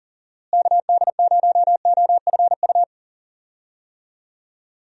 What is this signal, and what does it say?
Morse code, heard as a keyed tone, about 39 wpm, 700 Hz. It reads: KD0OFU
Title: USA Call Signs - 6 Characters - Repeated 2x - 40wpm